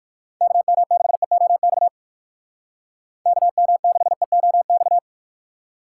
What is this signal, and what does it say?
Morse code, heard as a keyed tone, about 60 wpm, 700 Hz. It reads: KM6EOX KM6EOX